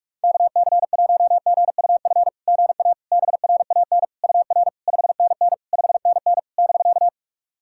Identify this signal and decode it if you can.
Morse code, heard as a keyed tone, about 44 wpm, 690 Hz. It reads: KC1GUF GA BRAN UR 5NN 5NN <BK>
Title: SOTA QSO Practice 45wpm